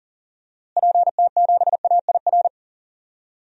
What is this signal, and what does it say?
Morse code, heard as a keyed tone, about 39 wpm, 700 Hz. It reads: PT7AIR
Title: DXCC-100 Call Signs - 6 Characters - Repeated 2x - 40wpm